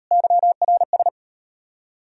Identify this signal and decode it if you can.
Morse code, heard as a keyed tone, about 38 wpm, 700 Hz. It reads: YRS